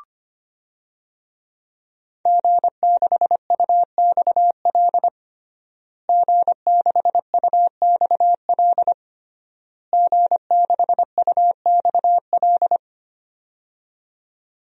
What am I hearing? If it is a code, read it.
Morse code, heard as a keyed tone, about 25 wpm, 700 Hz. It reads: G6UXL G6UXL G6UXL
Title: DXCC-100 Call Signs - 5 Characters - Repeated 3x - 25wpm